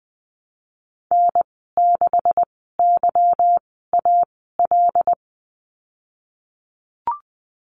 Morse code, heard as a keyed tone, about 20 wpm, spaced as for 14 wpm, 700 Hz. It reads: N6YAL
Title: 167 Lesson 27g - Number 7 - Call Signs 20wpm